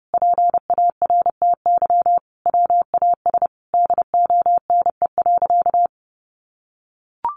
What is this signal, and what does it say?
Morse code, heard as a keyed tone, about 30 wpm, 700 Hz. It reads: PARTY WAS DONE.
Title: Sentences from Top 1400 Words 30wpm